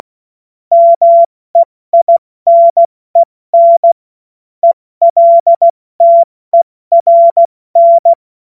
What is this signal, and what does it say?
Morse code, heard as a keyed tone, about 16 wpm, spaced as for 14 wpm, 680 Hz. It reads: MEINEN ELTERN
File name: Morsekurs / Witze / Witz 05 März 2024 70 F80